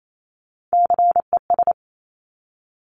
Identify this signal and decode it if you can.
Morse code, heard as a keyed tone, about 28 wpm, 690 Hz. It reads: CES
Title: Top Three Letter Combinations in Words 28wpm